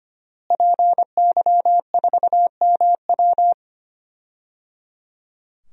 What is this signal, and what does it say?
Morse code, heard as a keyed tone, about 25 wpm, 700 Hz. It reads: PY4MW
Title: DXCC-100 Call Signs - 5 Characters - Repeated 3x - 25wpm